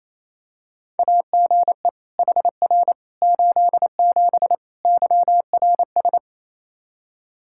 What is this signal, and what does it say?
Morse code, heard as a keyed tone, about 28 wpm, 700 Hz. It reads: AGE HR 87 YRS
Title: Practice QSO - Exchange 3 - More Info - 28wpm